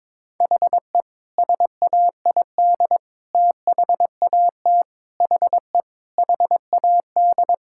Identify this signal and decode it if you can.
Morse code, heard as a keyed tone, about 21 wpm, 700 Hz. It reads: HE SAID THAT HE HAD
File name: practice / 22wpm / Sentences with 3-Word Phrases - 22wpm